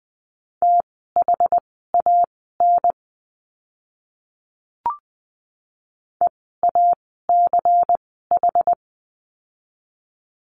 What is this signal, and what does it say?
Morse code, heard as a keyed tone, about 20 wpm, spaced as for 15 wpm, 700 Hz. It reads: THAN EACH
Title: 192 Lesson 31f - Letter v - Words 20wpm